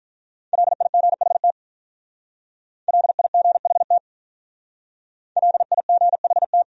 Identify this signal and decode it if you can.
Morse code, heard as a keyed tone, about 44 wpm, 700 Hz. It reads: LIGHT LIGHT LIGHT